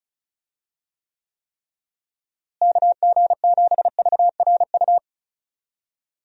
Morse code, heard as a keyed tone, about 36 wpm, 700 Hz. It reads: KG7VRU
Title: USA Call Signs - 6 Characters - Repeated 3x - 35wpm